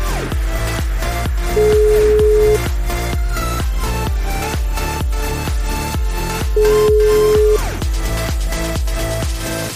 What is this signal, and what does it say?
Ringback tone (Europe, 425 Hz), with noise.